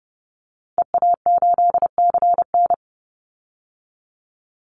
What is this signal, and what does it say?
Morse code, heard as a keyed tone, about 30 wpm, 700 Hz. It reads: EA8CN
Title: Call Signs Repeated 3x - 30wpm